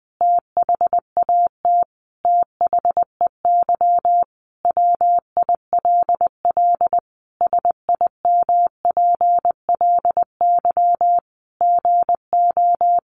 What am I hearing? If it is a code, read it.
Morse code, heard as a keyed tone, about 20 wpm, 700 Hz. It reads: THAT THEY WILL SIMPLY GO
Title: Sentences from Top 600 Words 20wpm